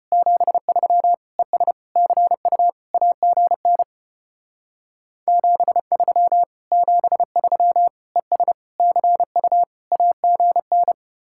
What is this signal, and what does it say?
Morse code, heard as a keyed tone, about 30 wpm, 700 Hz. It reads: 73 ES CU AGN 73 73 ES CU AGN
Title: Speed-Racing Common QSO Phrases Encore - 25wpm